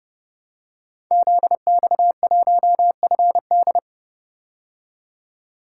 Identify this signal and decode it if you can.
Morse code, heard as a keyed tone, about 30 wpm, 700 Hz. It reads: ZX1FD
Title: Code Groups - 5 Letter:Numbers - 30wpm